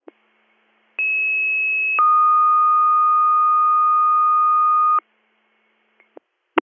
Two-tone page, heard as a one-second tone, then a three-second tone, about 2600 then 1230 Hz. It is heard from 1.0 to 5.0 s.